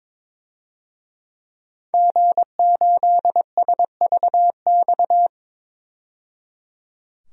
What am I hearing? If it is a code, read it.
Morse code, heard as a keyed tone, about 22 wpm, 700 Hz. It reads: G8SVX
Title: DXCC-100 Call Signs - 5 Characters - 22wpm